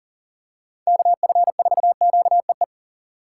Morse code, heard as a keyed tone, about 37 wpm, 700 Hz. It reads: KF4QEE
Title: USA Call Signs - 6 Characters - 6x Character Spacing - 40wpm